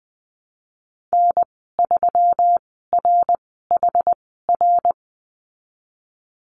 Morse code, heard as a keyed tone, about 20 wpm, spaced as for 15 wpm, 700 Hz. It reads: N3RHR
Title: 140 Lesson 23g - Letter f - Call Signs 20wpm